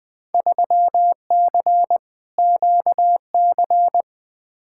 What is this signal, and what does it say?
Morse code, heard as a keyed tone, about 20 wpm, 700 Hz. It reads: 3C QC
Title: Field Day Exchanges Encore - 20wpm